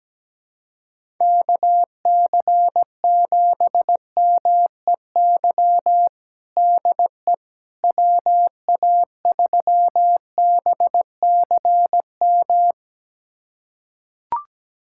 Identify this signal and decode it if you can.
Morse code, heard as a keyed tone, about 17 wpm, 700 Hz. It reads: KC7MEY DE WA3BCM
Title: USA Callsign DE Callsign 17wpm